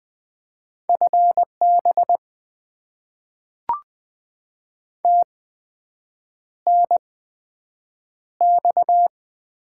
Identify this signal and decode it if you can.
Morse code, heard as a keyed tone, about 20 wpm, spaced as for 7 wpm, 700 Hz. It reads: FB TNX
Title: Top 20 QSO Elements - 8x Character Spacing - 20wpm